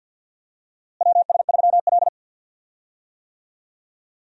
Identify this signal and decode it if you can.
Morse code, heard as a keyed tone, about 52 wpm, 700 Hz. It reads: WS3L
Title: USA Call Signs - 4 Characters - Repeated 2x - 50wpm